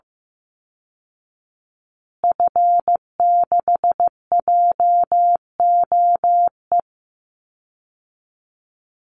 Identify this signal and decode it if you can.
Morse code, heard as a keyed tone, about 15 wpm, 700 Hz. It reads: F6JOE
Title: Call Signs Repeated 3x - 15wpm